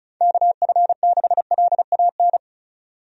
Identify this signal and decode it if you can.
Morse code, heard as a keyed tone, about 34 wpm, 700 Hz. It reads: KF6LAN
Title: USA Call Signs - 6 Characters - 35wpm